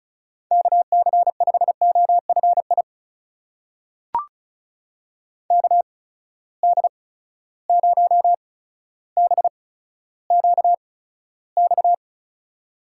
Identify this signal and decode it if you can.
Morse code, heard as a keyed tone, about 36 wpm, spaced as for 15 wpm, 700 Hz. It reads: KC5OFI KD0BQX
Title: USA Call Signs - 6 Characters - 8x Character Spacing - 35wpm